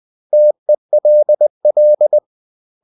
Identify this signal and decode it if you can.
Morse code, heard as a keyed tone, about 20 wpm, 600 Hz. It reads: TELL
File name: CW / Practice / 500MostCommonEnglishWords20